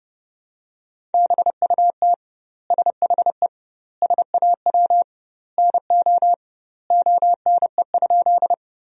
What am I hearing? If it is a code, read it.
Morse code, heard as a keyed tone, about 30 wpm, 700 Hz. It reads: BUT SHE SAW NO ONE?
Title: 2x Word-Spacing - Sentences from Top 300 Words - 30wpm